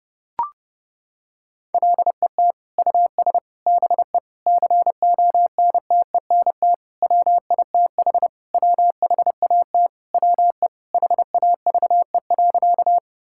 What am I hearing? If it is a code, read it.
Morse code, heard as a keyed tone, about 32 wpm, 700 Hz. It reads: LET US BE CONTENT WITH WHAT WE HAVE.
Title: Sentences from Top 1000 Words - Review Part 6 - 30wpm